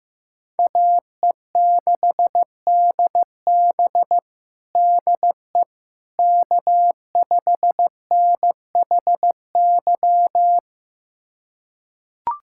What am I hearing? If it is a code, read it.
Morse code, heard as a keyed tone, about 16 wpm, 700 Hz. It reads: AE6DB DE K5NHY
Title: USA Callsign DE Callsign - 5 Characters - 15wpm